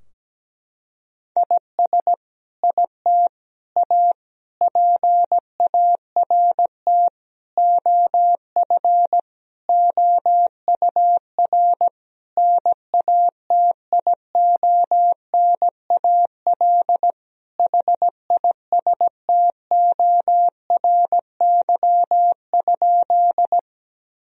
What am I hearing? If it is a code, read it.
Morse code, heard as a keyed tone, about 17 wpm, 700 Hz. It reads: IS IT A PART OF OUR NATIONAL HISTORY?
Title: Sentences from Top 1000 Words - Review Part 4 - 17wpm